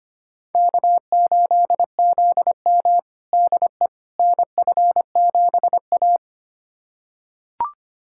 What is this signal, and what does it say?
Morse code, heard as a keyed tone, about 25 wpm, 700 Hz. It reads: K8ZM DE NF7A
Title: USA Callsign DE Callsign - 4 Characters - 25wpm